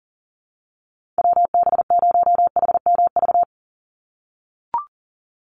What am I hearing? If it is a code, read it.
Morse code, heard as a keyed tone, about 40 wpm, 700 Hz. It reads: WB0HMV
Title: USA Call Signs 40wpm